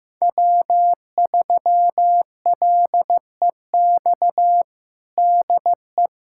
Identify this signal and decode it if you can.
Morse code, heard as a keyed tone, about 15 wpm, 700 Hz. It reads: W3LEX DE
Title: USA Callsign DE Callsign - 5 Characters - 15wpm